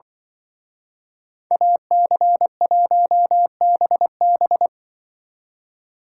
Morse code, heard as a keyed tone, about 24 wpm, 700 Hz. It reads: AC1BB
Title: Speed Racing - USA 2x2 Call Signs 20wpm